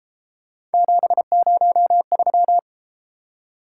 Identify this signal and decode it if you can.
Morse code, heard as a keyed tone, about 31 wpm, 700 Hz. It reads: 703